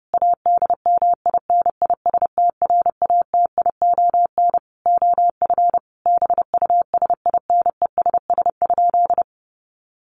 Morse code, heard as a keyed tone, about 30 wpm, 700 Hz. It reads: ADMINISTRATION OF BUSINESS?
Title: Sentences from Top 1300 Words 30wpm